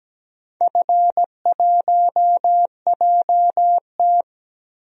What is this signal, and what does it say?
Morse code, heard as a keyed tone, about 17 wpm, 700 Hz. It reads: F1JT
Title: DXCC-100 Call Signs - 4 Characters - 17wpm